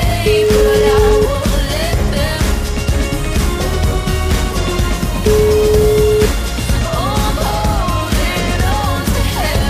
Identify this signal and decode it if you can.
Ringback tone (Europe, 425 Hz), with noise.